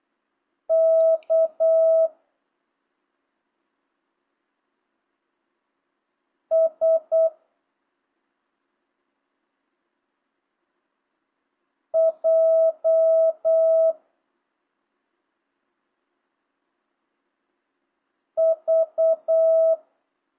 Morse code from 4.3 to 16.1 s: S J